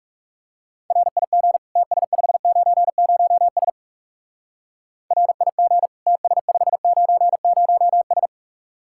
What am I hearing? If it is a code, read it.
Morse code, heard as a keyed tone, about 38 wpm, 700 Hz. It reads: RIG TS590S RIG TS590S